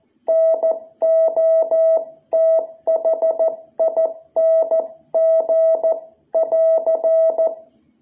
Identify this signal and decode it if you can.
Morse code, heard as a keyed tone, about 14 wpm, 660 Hz. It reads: NOTHING <AR>
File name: jh8cbh / 2021cw